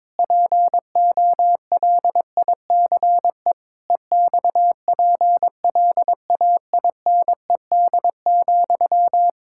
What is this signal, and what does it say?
Morse code, heard as a keyed tone, about 22 wpm, 700 Hz. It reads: POLICE EXPLAINED,
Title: Pangram Challenge v2 - 22wpm